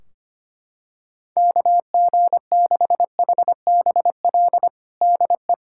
Morse code, heard as a keyed tone, about 25 wpm, 700 Hz. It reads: KG6HBL DE
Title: K1USN SST QSO Practice 25wpm